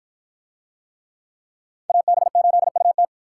Morse code, heard as a keyed tone, about 53 wpm, 700 Hz. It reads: AB8UT